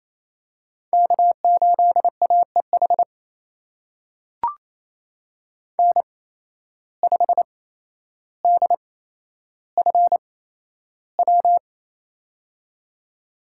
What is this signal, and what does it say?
Morse code, heard as a keyed tone, about 27 wpm, spaced as for 12 wpm, 700 Hz. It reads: K8AEH N5DFW
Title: USA Call Signs - 5 Characters - 8x Character Spacing - 28wpm